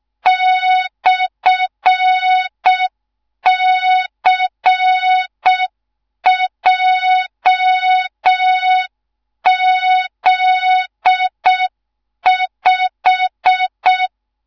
Morse code, heard as a keyed tone, about 6 wpm, 3700 Hz. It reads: /CJZ5
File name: Morze-HA2MN / hang / 13IRASJELEK1 HA2MN